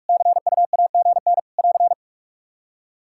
Morse code, heard as a keyed tone, about 46 wpm, 700 Hz. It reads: CUAGN <AR>
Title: Practice QSO - Exchange 4 - Ending - 45wpm